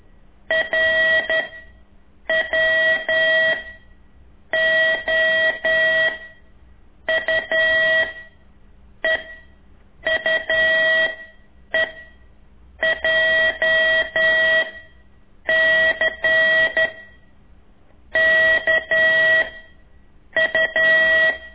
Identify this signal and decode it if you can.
Morse code, heard as a keyed tone, about 11 wpm, 1950 Hz, with noise. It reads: R W O U E U E J C K U
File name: ik1weg2 / CORSI CW